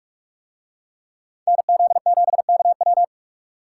Morse code, heard as a keyed tone, about 44 wpm, 700 Hz. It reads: NZ7KW